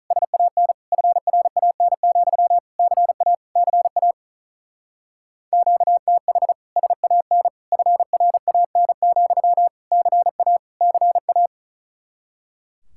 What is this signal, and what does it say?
Morse code, heard as a keyed tone, about 42 wpm, 700 Hz. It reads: SAN FRAN, CA CA QTH SAN FRAN, CA CA
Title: Speed-Racing Common QSO Phrases - 35wpm